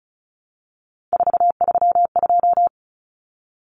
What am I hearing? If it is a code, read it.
Morse code, heard as a keyed tone, about 34 wpm, 700 Hz. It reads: 432
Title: Common Cut Numbers - 3-Digit 35wpm